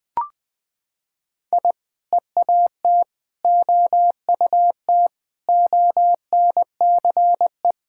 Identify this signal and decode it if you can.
Morse code, heard as a keyed tone, about 19 wpm, 700 Hz. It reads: I EAT OUT ONCE
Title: Sentences with 4-Word Phrases - 20wpm